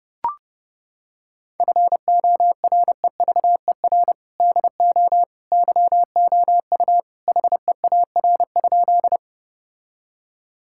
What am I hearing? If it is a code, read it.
Morse code, heard as a keyed tone, about 32 wpm, 700 Hz. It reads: FOREVER DO YOU HEAR?
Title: Sentences from Top 1600 Words 30wpm